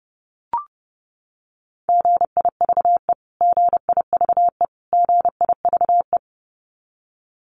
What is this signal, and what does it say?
Morse code, heard as a keyed tone, about 30 wpm, 700 Hz. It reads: GIVE GIVE GIVE
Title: Top 300 Words Repeated 30wpm